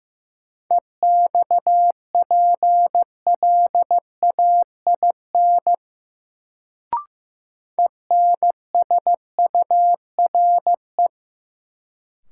Morse code, heard as a keyed tone, about 15 wpm, 700 Hz. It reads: EXPLAIN ENSURE
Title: Sentences from Top 1100 Words - New Word Drill Alt - 15wpm